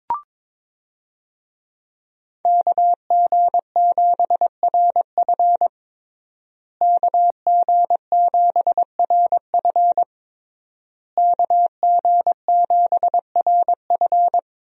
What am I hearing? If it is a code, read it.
Morse code, heard as a keyed tone, about 22 wpm, 700 Hz. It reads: KG7RF KG7RF KG7RF
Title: USA Call Signs - 5 Characters - Repeated 3x - 22wpm